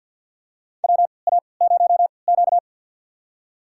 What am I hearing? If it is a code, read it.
Morse code, heard as a keyed tone, about 53 wpm, spaced as for 29 wpm, 700 Hz. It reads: WA0Q
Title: USA Call Signs - 4 Characters - 3x Character Spacing - 50wpm